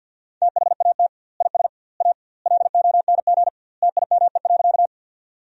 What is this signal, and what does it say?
Morse code, heard as a keyed tone, about 50 wpm, 700 Hz. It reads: THAT IS A LONG TIME.
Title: 2x Word-Spacing - Sentences from Top 100 Words - 50wpm